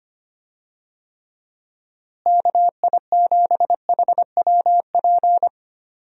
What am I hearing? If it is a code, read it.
Morse code, heard as a keyed tone, about 25 wpm, 700 Hz. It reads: KI7HWP
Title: USA Call Signs - 6 Characters - Repeated 3x - 25wpm